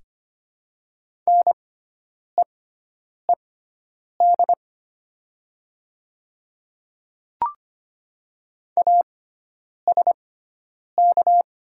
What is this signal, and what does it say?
Morse code, heard as a keyed tone, about 25 wpm, spaced as for 9 wpm, 700 Hz. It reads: NEED ASK
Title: 6x Character Spacing - Top 200 Words - 25wpm